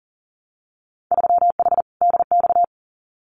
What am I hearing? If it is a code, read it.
Morse code, heard as a keyed tone, about 40 wpm, 700 Hz. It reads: 3H DX
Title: Winter Field Day Exchanges Encore 40wpm